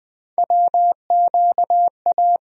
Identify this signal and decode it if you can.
Morse code, heard as a keyed tone, about 20 wpm, 700 Hz. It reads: WQA
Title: USA Call Sign Postfixes - 3 Letters 20wpm